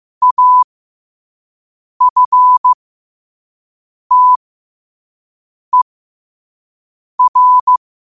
Morse code, heard as a keyed tone, about 15 wpm, spaced as for 5 wpm, 1000 Hz. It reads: AFTER
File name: qotdfiles / 5wpm